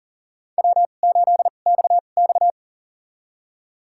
Morse code, heard as a keyed tone, about 39 wpm, spaced as for 29 wpm, 700 Hz. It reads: W8XX
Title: USA Call Signs - 2x Character Spacing - 40wpm